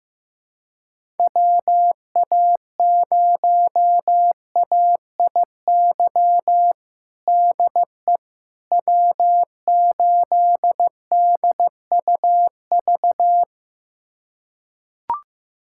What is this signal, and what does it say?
Morse code, heard as a keyed tone, about 15 wpm, 700 Hz. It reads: WA0AIY DE W8DUV
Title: USA Callsign DE Callsign 15wpm